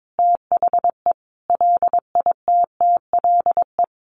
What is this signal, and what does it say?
Morse code, heard as a keyed tone, about 22 wpm, 700 Hz. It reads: THE LITTLE
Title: Sentences from Top 1000 Words 22wpm